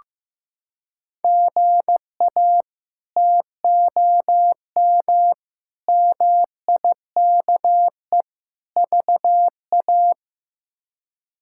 Morse code, heard as a keyed tone, about 15 wpm, 700 Hz. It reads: GA TOM MIKE VA
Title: K1USN SST QSO Practice - 15wpm